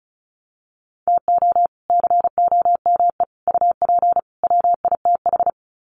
Morse code, heard as a keyed tone, about 34 wpm, 700 Hz. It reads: TO COME UP WITH
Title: Sets of 4 Words 35wpm